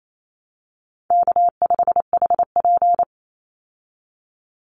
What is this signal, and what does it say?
Morse code, heard as a keyed tone, about 28 wpm, 700 Hz. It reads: K5HP
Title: Call Signs Repeated 2x - 28wpm